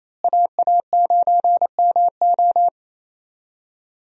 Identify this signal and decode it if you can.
Morse code, heard as a keyed tone, about 28 wpm, 700 Hz. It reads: AA9MO
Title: USA Call Signs - 5 Characters - Repeated 2x - 28wpm